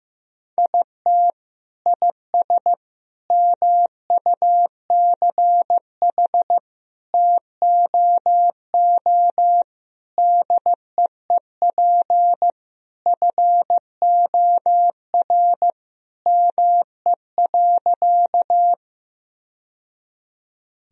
Morse code, heard as a keyed tone, about 15 wpm, 700 Hz. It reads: IT IS MUCH TOO DEEP FOR ME.
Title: Sentences from Top 1000 Words - Review Part 3 - 15wpm